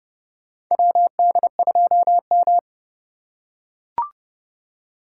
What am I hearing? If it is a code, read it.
Morse code, heard as a keyed tone, about 31 wpm, 700 Hz. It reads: WD2M
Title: USA Call Signs - 4 Characters - 8x Character Spacing - 30wpm